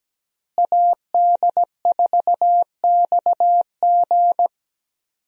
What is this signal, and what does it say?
Morse code, heard as a keyed tone, about 17 wpm, 700 Hz. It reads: AD4XG
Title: USA Call Signs - 5 Characters - Repeated 2x - 17wpm